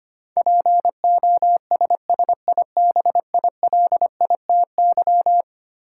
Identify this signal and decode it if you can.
Morse code, heard as a keyed tone, about 25 wpm, 700 Hz. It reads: POSSIBILITY
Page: morse-code-ninja-practice-25wpm